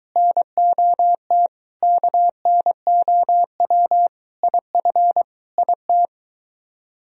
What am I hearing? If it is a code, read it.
Morse code, heard as a keyed tone, about 22 wpm, 700 Hz. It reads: NOT KNOW IF IT